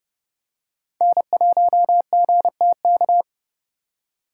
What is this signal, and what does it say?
Morse code, heard as a keyed tone, about 30 wpm, 700 Hz. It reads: N1GTK